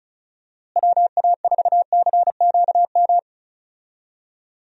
Morse code, heard as a keyed tone, about 34 wpm, 700 Hz. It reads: WA4CQM
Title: USA Call Signs - 6 Characters - Repeated 2x - 35wpm